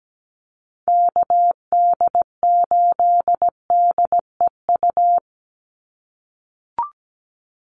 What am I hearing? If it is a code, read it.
Morse code, heard as a keyed tone, about 17 wpm, 700 Hz. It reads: KD8DEU
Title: NA SOTA Chaser Callsigns 17wpm